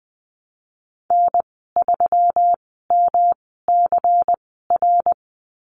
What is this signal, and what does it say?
Morse code, heard as a keyed tone, about 20 wpm, spaced as for 14 wpm, 700 Hz. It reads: N3MCR
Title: 140 Lesson 23g - Letter f - Call Signs 20wpm